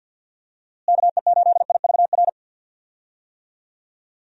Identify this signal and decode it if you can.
Morse code, heard as a keyed tone, about 45 wpm, 700 Hz. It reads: KE8IVR
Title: USA Call Signs - 6 Characters - Repeated 2x - 50wpm